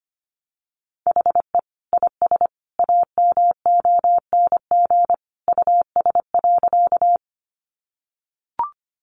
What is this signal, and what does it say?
Morse code, heard as a keyed tone, about 25 wpm, 700 Hz. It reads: HE IS AMONG US.
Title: Sentences from Top 500 Words 25wpm